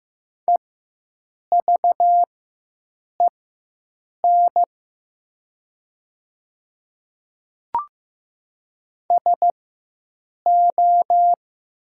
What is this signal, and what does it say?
Morse code, heard as a keyed tone, about 15 wpm, spaced as for 7 wpm, 700 Hz. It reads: EVEN SO
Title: 4x Character Spacing - Top 300 Words - 15wpm